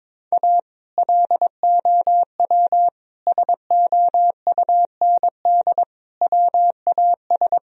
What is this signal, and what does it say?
Morse code, heard as a keyed tone, about 22 wpm, 700 Hz. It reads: A LOW SOUND WAS
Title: Sentences from Top 1000 Words - Review Part 1 - 22wpm